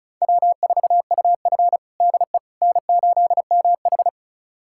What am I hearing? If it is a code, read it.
Morse code, heard as a keyed tone, about 34 wpm, 700 Hz. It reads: W4UF DE N8MH
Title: USA Callsign DE Callsign - 4 Characters - 35wpm